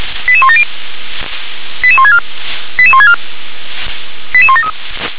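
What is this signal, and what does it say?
Five-tone selective call (ZVEI), heard as quick runs of five tones, tones of about 70 ms. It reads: 88170 88164 88164 88173
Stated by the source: ZVEI five-tone call received by SDR.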